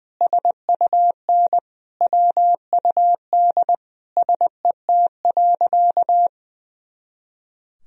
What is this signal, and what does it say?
Morse code, heard as a keyed tone, about 20 wpm, 700 Hz. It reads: SUN WUD SET.
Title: Sentences from Top 400 Words and Abbreviations 20wpm